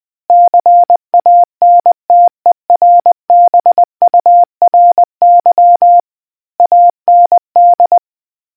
Morse code, heard as a keyed tone, about 20 wpm, 700 Hz. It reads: CANTERBURY AND